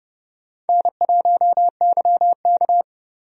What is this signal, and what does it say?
Morse code, heard as a keyed tone, about 30 wpm, 700 Hz. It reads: N1YK
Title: USA Call Signs - 4 Characters - 30wpm